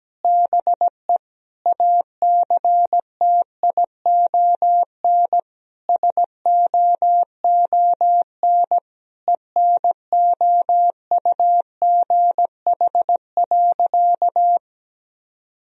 Morse code, heard as a keyed tone, about 17 wpm, 700 Hz. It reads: BE ACTION SOON ENOUGH.
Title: Sentences from Top 1000 Words - Review Part 4 - 17wpm